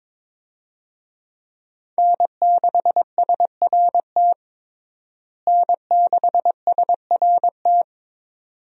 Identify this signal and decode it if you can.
Morse code, heard as a keyed tone, about 22 wpm, 700 Hz. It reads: N6SRT N6SRT
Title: USA Call Signs - 5 Characters - Repeated 2x - 22wpm